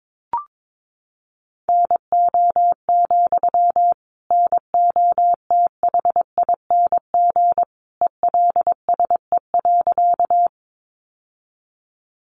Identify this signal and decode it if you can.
Morse code, heard as a keyed tone, about 22 wpm, 700 Hz. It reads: NO, NOTHING ELSE.
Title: Sentences from Top 800 Words 22wpm